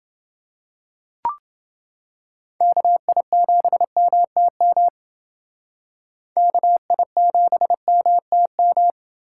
Morse code, heard as a keyed tone, about 29 wpm, 700 Hz. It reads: KI7MTM KI7MTM